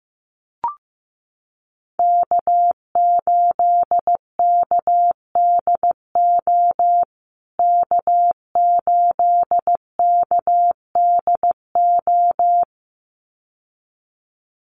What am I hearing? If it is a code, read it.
Morse code, heard as a keyed tone, about 15 wpm, 700 Hz. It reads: K8KDO K8KDO
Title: USA Call Signs Repeated 2x 15wpm